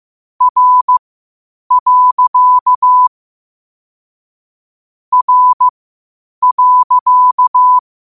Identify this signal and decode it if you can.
Morse code, heard as a keyed tone, about 15 wpm, spaced as for 8 wpm, 1000 Hz. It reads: R. R.
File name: qotdfiles / 8wpm